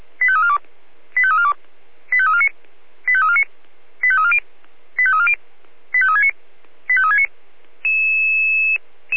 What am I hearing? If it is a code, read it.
Five-tone selective call (ZVEI), heard as quick runs of five tones, tones of about 70 ms. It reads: 85342 85342 85349 85349 85340 85340 85369 85369
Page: SDR/ZVEI